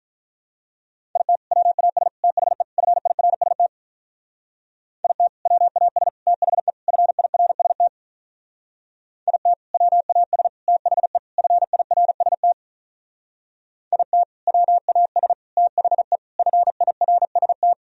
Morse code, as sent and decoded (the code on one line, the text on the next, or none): .. - / .-- .- ... / - .... / ..-. .. .-. ... - / .. - / .-- .- ... / - .... / ..-. .. .-. ... - / .. - / .-- .- ... / - .... / ..-. .. .-. ... - / .. - / .-- .- ... / - .... / ..-. .. .-. ... -
IT WAS THE FIRST IT WAS THE FIRST IT WAS THE FIRST IT WAS THE FIRST